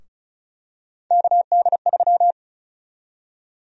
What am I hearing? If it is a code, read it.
Morse code, heard as a keyed tone, about 36 wpm, 700 Hz. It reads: KD3
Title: USA Call Sign Prefixes - Repeated 2x - 35wpm